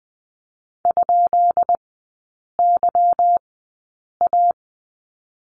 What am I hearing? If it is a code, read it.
Morse code, heard as a keyed tone, about 20 wpm, spaced as for 8 wpm, 700 Hz. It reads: ?YA
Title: 205 Lesson 33e - Letter j - Three Characters 20wpm